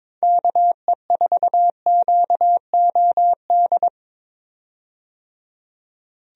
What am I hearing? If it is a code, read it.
Morse code, heard as a keyed tone, about 22 wpm, 700 Hz. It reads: KE4QOD
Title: USA Call Signs - 6 Characters - 22wpm